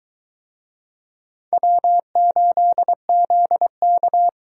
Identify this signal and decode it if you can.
Morse code, heard as a keyed tone, about 23 wpm, 700 Hz. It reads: W8ZK